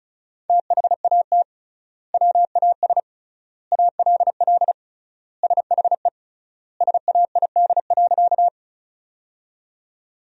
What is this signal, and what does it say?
Morse code, heard as a keyed tone, about 36 wpm, 700 Hz. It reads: THAT WAS ALL SHE SAID.
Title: 3x Word-Spacing - Sentences from Top 100 Words - 35wpm